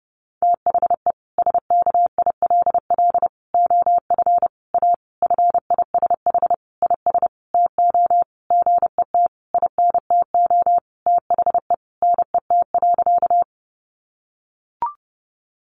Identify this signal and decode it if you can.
Morse code, heard as a keyed tone, about 30 wpm, 700 Hz. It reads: THE SKILL OF A FISH IS TO GET INTO THE NET.
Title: Sentences from Top 1400 Words 30wpm